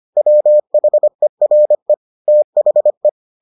Morse code, heard as a keyed tone, about 25 wpm, 600 Hz. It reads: WHERE THE